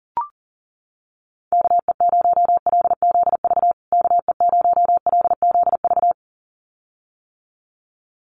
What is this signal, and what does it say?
Morse code, heard as a keyed tone, about 40 wpm, 700 Hz. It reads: KE0LZV KE0LZV
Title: USA Call Signs Repeated 2x 40wpm